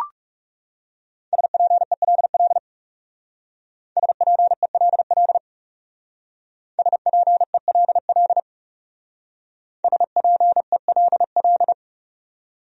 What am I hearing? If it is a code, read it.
Morse code, heard as a keyed tone, about 38 wpm, 700 Hz. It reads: SPELL SPELL SPELL SPELL